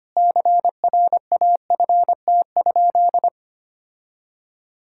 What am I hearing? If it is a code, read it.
Morse code, heard as a keyed tone, about 25 wpm, 700 Hz. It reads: CRAFT?
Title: Sentences from Top 1000 Words - Review Part 6 - 25wpm